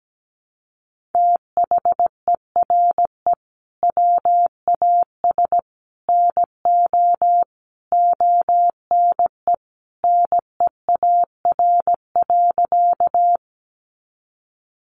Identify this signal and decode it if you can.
Morse code, heard as a keyed tone, about 17 wpm, 700 Hz. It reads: THERE WAS NO ONE NEAR.
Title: Sentences from Top 200 Words 17wpm